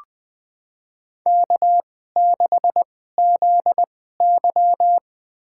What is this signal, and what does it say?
Morse code, heard as a keyed tone, about 20 wpm, spaced as for 15 wpm, 700 Hz. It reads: K6ZY
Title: USA Call Signs - 4 Characters - 2x Character Spacing - 20wpm